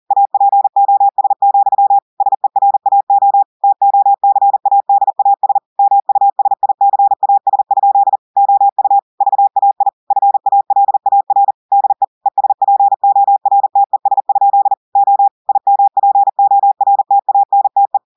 Morse code, heard as a keyed tone, about 40 wpm, 800 Hz. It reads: APOS, SERAO TOCADAS MUSICAS? OU VAI FALAR DE ESPORTES? O IMPORTANTE
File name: audios / portugais / portugais texte en clair 2 vitesse 40 mots minutes